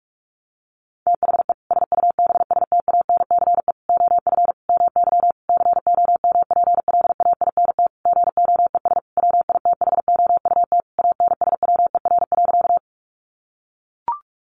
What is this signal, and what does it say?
Morse code, heard as a keyed tone, about 44 wpm, 700 Hz. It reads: THE SUBSTANCE OF MY COMPLAINT GOES WITHOUT ANSWER.
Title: Sentences from Top 1500 Words 45wpm